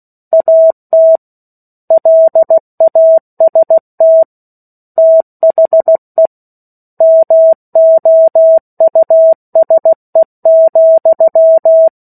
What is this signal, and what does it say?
Morse code, heard as a keyed tone, about 16 wpm, 650 Hz. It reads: AT LAST THE MOUSE,
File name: 16 WPM